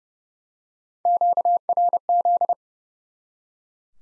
Morse code, heard as a keyed tone, about 30 wpm, 700 Hz. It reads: QRZ
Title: Top 90 Abbr 30WPM